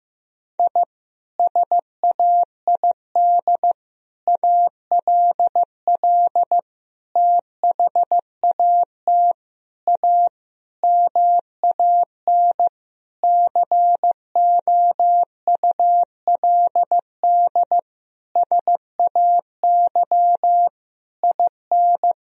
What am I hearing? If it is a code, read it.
Morse code, heard as a keyed tone, about 16 wpm, 700 Hz. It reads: I SAID ALL THAT A MAN COULD SAY IN
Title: Sentences from Top 1000 Words - Review Part 2 - 15wpm